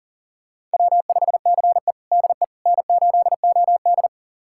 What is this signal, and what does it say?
Morse code, heard as a keyed tone, about 37 wpm, 700 Hz. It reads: W5CE DE N8OD
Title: USA Callsign DE Callsign - 4 Characters - 40wpm